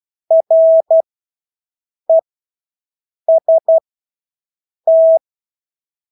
Morse code, heard as a keyed tone, about 12 wpm, spaced as for 6 wpm, 650 Hz. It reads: REST
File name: Engelske ord 12-6 wpm